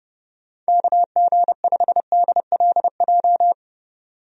Morse code, heard as a keyed tone, about 30 wpm, 700 Hz. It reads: KG5DLJ
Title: USA Call Signs - 6 Characters - 30wpm